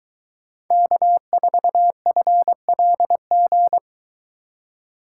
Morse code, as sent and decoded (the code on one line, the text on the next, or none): -.- ....- ..-. .-.. --.
K4FLG